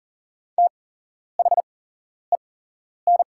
Morse code, heard as a keyed tone, about 39 wpm, spaced as for 11 wpm, 700 Hz. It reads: THEN
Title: 8x Character Spacing - Top 500 Words - 40wpm